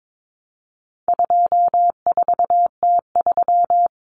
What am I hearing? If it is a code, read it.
Morse code, heard as a keyed tone, about 22 wpm, 700 Hz. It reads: 24T3
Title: Common Cut Numbers - 4-Digit 22wpm